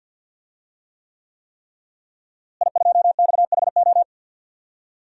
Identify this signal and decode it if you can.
Morse code, heard as a keyed tone, about 46 wpm, 700 Hz. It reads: I2XHO
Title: DXCC-100 Call Signs - 5 Characters - Repeated 3x - 50wpm